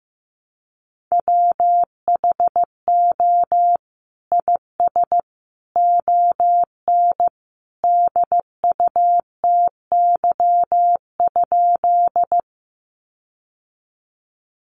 Morse code, heard as a keyed tone, about 15 wpm, 700 Hz. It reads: WHO IS ON DUTY?
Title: Sentences from Top 1500 Words 15wpm